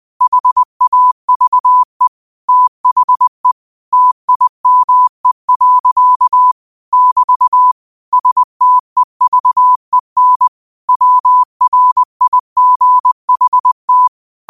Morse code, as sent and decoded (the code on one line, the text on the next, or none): .... .- ...- . / - .... / - .. -- . .-.-.- / -...- / ... - . ...- . -. / .-- .-. .. --. .... -
HAVE THE TIME. <BT> STEVEN WRIGHT